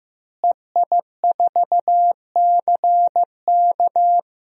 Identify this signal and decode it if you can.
Morse code, heard as a keyed tone, about 15 wpm, 700 Hz. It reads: EI4CK
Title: DXCC-100 Call Signs - 5 Characters - Repeated 3x - 15wpm